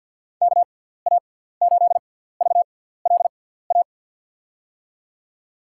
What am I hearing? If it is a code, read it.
Morse code, heard as a keyed tone, about 53 wpm, spaced as for 17 wpm, 700 Hz. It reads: KA8VLA
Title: USA Call Signs - 6x Character Spacing - 50wpm